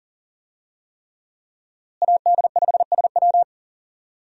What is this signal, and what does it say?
Morse code, heard as a keyed tone, about 40 wpm, 700 Hz. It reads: AD5SW
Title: USA Call Signs - 5 Characters - Repeated 3x - 40wpm